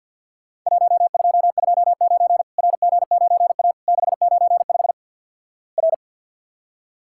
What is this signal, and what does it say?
Morse code, heard as a keyed tone, about 50 wpm, 700 Hz. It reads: 1229 RG9A 695 R
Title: Worked All Europe DX Contest - QTC Exchange - 50wpm